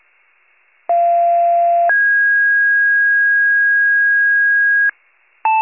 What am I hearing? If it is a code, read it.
Two-tone page, heard as a one-second tone, then a three-second tone, about 690 then 1650 Hz. It is heard from 0.9 to 4.9 s.